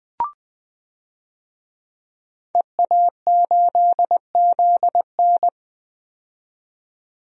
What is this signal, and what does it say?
Morse code, heard as a keyed tone, about 20 wpm, 700 Hz. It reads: EA8ZN
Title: DXCC-100 Call Signs - 5 Characters - 20wpm